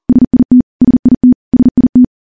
Morse code, heard as a keyed tone, about 39 wpm, 250 Hz. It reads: SIT SIT SIT
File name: morse / morse-training-40WPM